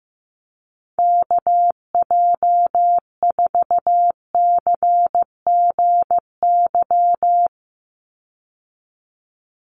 Morse code, heard as a keyed tone, about 15 wpm, 700 Hz. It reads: KJ4CGY
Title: USA Call Signs 15wpm